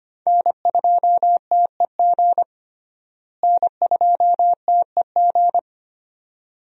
Morse code, heard as a keyed tone, about 25 wpm, 700 Hz. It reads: N2TEG N2TEG
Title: USA Call Signs - 5 Characters - Repeated 2x - 25wpm